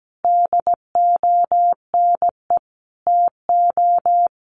Morse code, heard as a keyed tone, about 17 wpm, 700 Hz. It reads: DONE TO
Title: A Sound Of Thunder 17wpm